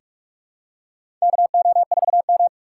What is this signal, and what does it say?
Morse code, heard as a keyed tone, about 46 wpm, 690 Hz. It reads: KO4M
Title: USA Call Signs - 4x Character Spacing - 45wpm